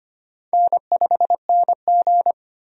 Morse code, heard as a keyed tone, about 25 wpm, 700 Hz. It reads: N5NG
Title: USA Call Signs - 4 Characters - 25wpm